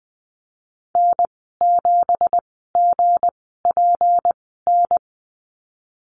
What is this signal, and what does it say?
Morse code, heard as a keyed tone, about 19 wpm, spaced as for 14 wpm, 700 Hz. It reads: N7GPN
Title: 200 Lesson 32g - Letter k - Call Signs 20wpm